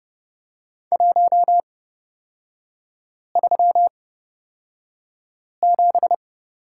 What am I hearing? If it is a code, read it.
Morse code, heard as a keyed tone, about 31 wpm, spaced as for 5 wpm, 700 Hz. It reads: 137
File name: Lesson 37e - Letter x - Three Characters 30wpm